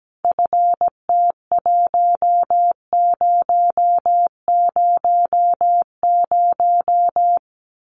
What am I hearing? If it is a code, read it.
Morse code, heard as a keyed tone, about 17 wpm, 700 Hz. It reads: FT1000
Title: Common QSO Phrases 17wpm